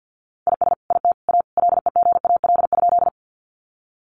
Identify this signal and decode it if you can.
Morse code, heard as a keyed tone, about 50 wpm, 700 Hz. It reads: IS IT A LEGAL?
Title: Sentences from Top 1200 Words 50wpm